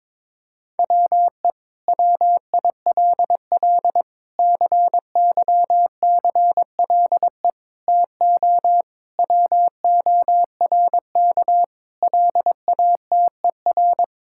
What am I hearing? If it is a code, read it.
Morse code, heard as a keyed tone, about 22 wpm, 700 Hz. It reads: WE WILL CYCLE TO WORK LATER
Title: Sentences from Top 1000 Words - Review Part 6 - 22wpm